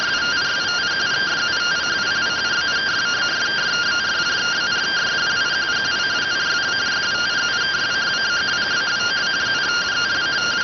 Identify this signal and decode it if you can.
Two-tone teleprinter signal, about 170 Hz shift.